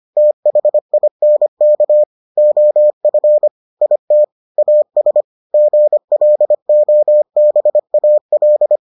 Morse code, heard as a keyed tone, about 25 wpm, 600 Hz. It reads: THINK OF IT AS GLOBAL